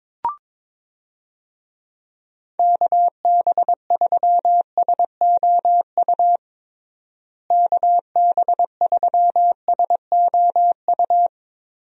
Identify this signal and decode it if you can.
Morse code, heard as a keyed tone, about 22 wpm, 700 Hz. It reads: KB3SOU KB3SOU
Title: USA Call Signs - 6 Characters - Repeated 3x - 22wpm